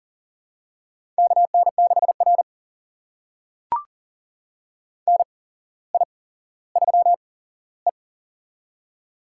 Morse code, heard as a keyed tone, about 42 wpm, spaced as for 17 wpm, 700 Hz. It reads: KN6R NI3E
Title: USA Call Signs - 4 Characters - 8x Character Spacing - 40wpm